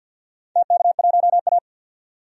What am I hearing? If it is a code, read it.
Morse code, heard as a keyed tone, about 46 wpm, 700 Hz. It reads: TK1A
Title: DXCC-100 Call Signs - 4 Characters - 50wpm